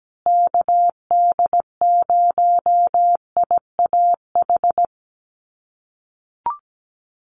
Morse code, heard as a keyed tone, about 17 wpm, 700 Hz. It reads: KD0IAH
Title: USA Call Signs Repeated 2x 17wpm